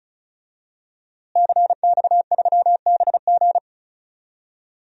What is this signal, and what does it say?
Morse code, heard as a keyed tone, about 36 wpm, 700 Hz. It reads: CX3BG
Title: DXCC-100 Call Signs - 5 Characters - Repeated 2x - 35wpm